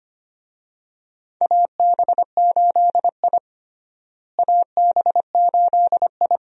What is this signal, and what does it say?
Morse code, heard as a keyed tone, about 25 wpm, 700 Hz. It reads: AB8I AB8I
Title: USA Call Signs - 4 Characters - Repeated 2x - 25wpm